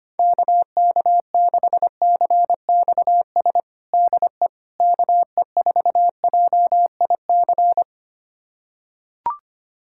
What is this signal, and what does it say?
Morse code, heard as a keyed tone, about 25 wpm, 700 Hz. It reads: KK6CXS DE KE4JIC
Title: USA Callsign DE Callsign - 6 Characters - 25wpm